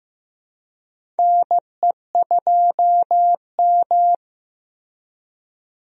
Morse code, heard as a keyed tone, about 16 wpm, 700 Hz. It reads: NE2M
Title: USA Call Signs - 4 Characters - Repeated 2x - 15wpm